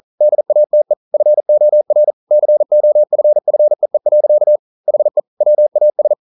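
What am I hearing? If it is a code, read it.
Morse code, heard as a keyed tone, about 44 wpm, 600 Hz. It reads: DATE FOR COFFEE. HE WAS
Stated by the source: Morse code at 40 wpm, practice text, CW med Gnister.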